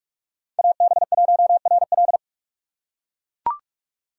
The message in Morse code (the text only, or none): AB1RL